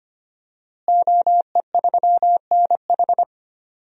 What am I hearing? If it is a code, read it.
Morse code, heard as a keyed tone, about 25 wpm, 700 Hz. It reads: OE3NH
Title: DXCC-100 Call Signs - 5 Characters - Repeated 2x - 25wpm